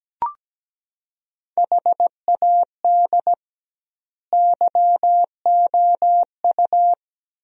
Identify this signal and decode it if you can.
Morse code, heard as a keyed tone, about 17 wpm, 700 Hz. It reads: HAD YOU
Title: 2x Word-Spacing - Sentences from Top 400 Words - 17wpm